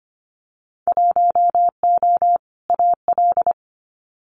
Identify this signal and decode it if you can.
Morse code, heard as a keyed tone, about 25 wpm, 700 Hz. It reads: 1O AL
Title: Winter Field Day Exchanges Encore 25wpm